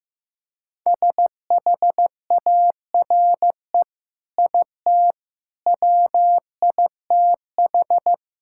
Morse code, heard as a keyed tone, about 15 wpm, 700 Hz. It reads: SHARE IT WITH
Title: Sentences from Top 1000 Words - Review Part 6 - 15wpm